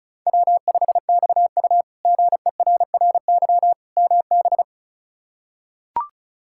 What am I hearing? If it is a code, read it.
Morse code, heard as a keyed tone, about 36 wpm, 700 Hz. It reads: W5XU GERRY MB
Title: K1USN SST QSO Practice - 35wpm